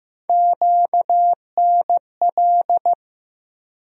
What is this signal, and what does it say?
Morse code, heard as a keyed tone, about 15 wpm, 700 Hz. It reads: QNL
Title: USA Call Sign Postfixes - 3 Letters 15wpm